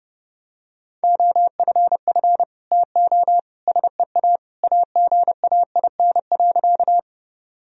Morse code, heard as a keyed tone, about 30 wpm, 700 Hz. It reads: OFF TO SEA AGAIN.
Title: Sentences from Top 1000 Words - Review Part 1 - 30wpm